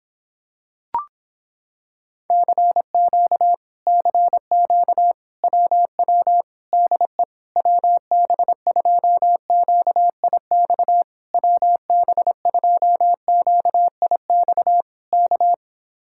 Morse code, heard as a keyed tone, about 26 wpm, 700 Hz. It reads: CQ CQ WW DE WB2QIX WB2QIX K